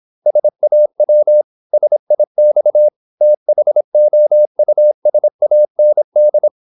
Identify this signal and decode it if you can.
Morse code, heard as a keyed tone, about 26 wpm, 600 Hz. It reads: SAW SIX THOUSAND
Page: Never 26wpm | CW med Gnister